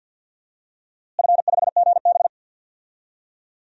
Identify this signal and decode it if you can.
Morse code, heard as a keyed tone, about 47 wpm, 700 Hz. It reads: F5GB
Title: DXCC-100 Call Signs - 4 Characters - Repeated 3x - 50wpm